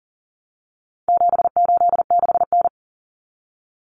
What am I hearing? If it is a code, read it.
Morse code, heard as a keyed tone, about 40 wpm, 700 Hz. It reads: 786N
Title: Common Cut Numbers - 4-Digit 40wpm